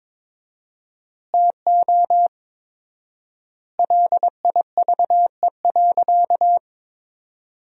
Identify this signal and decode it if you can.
Morse code, heard as a keyed tone, about 23 wpm, 700 Hz. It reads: TO LIVE.
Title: 4x Word-Spacing - Sentences from Top 300 Words - 22wpm